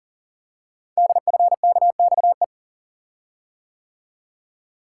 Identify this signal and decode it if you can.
Morse code, heard as a keyed tone, about 42 wpm, 700 Hz. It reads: DFKXE
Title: Code Groups - 5 Letters - 40wpm